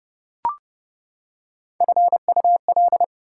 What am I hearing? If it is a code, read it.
Morse code, heard as a keyed tone, about 29 wpm, 700 Hz. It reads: FUL
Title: Prefixes and Suffixes 30wpm